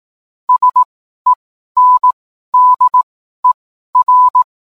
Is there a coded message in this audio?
Yes: Morse code from 0.5 to 4.4 s, heard as a keyed tone, about 18 wpm, spaced as for 12 wpm, 1000 Hz. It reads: SENDER